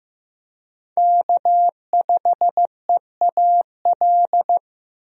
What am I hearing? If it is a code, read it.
Morse code, heard as a keyed tone, about 15 wpm, 700 Hz. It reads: K5EAL
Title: USA Call Signs - 5 Characters - Repeated 3x - 15wpm